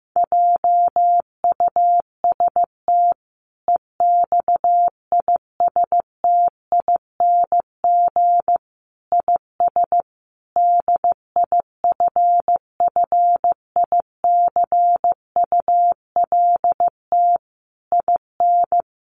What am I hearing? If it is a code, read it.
Morse code, heard as a keyed tone, about 15 wpm, 700 Hz. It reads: JUST EXISTING IS DIFFICULT IN
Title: Sentences from Top 1500 Words 15wpm